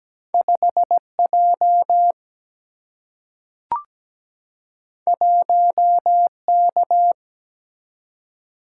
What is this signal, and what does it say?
Morse code, heard as a keyed tone, about 17 wpm, 700 Hz. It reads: 5 J 1 K
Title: Single-Letter Single-Number 17wpm